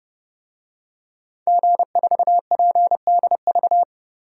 Morse code, heard as a keyed tone, about 30 wpm, 700 Hz. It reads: G4PDV
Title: DXCC-100 Call Signs - 5 Characters - 30wpm